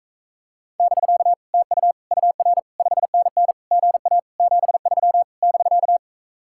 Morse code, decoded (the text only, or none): <BK> TU UR 5NN GA 73 <BK>